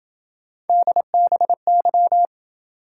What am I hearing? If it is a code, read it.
Morse code, heard as a keyed tone, about 27 wpm, 700 Hz. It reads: DBY